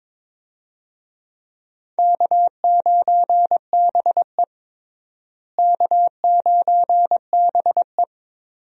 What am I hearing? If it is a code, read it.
Morse code, heard as a keyed tone, about 22 wpm, 700 Hz. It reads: K9BE K9BE
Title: USA Call Signs - 4 Characters - Repeated 3x - 22wpm